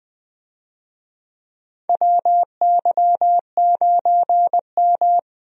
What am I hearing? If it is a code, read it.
Morse code, heard as a keyed tone, about 21 wpm, 700 Hz. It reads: WY9M
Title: USA Call Signs - 4 Characters - 20wpm